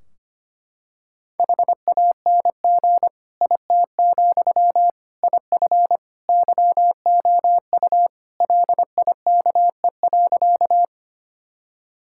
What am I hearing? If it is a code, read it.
Morse code, heard as a keyed tone, about 25 wpm, 700 Hz. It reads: HANG IT, IF YOU LIKE.
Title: Sentences from Top 1600 Words 25wpm